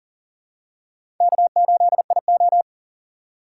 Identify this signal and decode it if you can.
Morse code, heard as a keyed tone, about 40 wpm, 700 Hz. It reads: K8IO
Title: USA Call Signs - 4 Characters - 4x Character Spacing - 40wpm